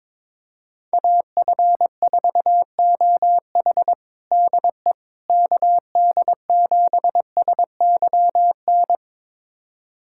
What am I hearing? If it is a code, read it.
Morse code, heard as a keyed tone, about 22 wpm, 700 Hz. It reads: AF4OH DE KD7SYN
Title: USA Callsign DE Callsign 22wpm